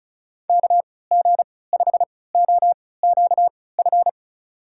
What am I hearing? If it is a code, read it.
Morse code, heard as a keyed tone, about 36 wpm, spaced as for 20 wpm, 700 Hz. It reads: KG5OQF
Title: USA Call Signs - 6 Characters - 3x Character Spacing - 35wpm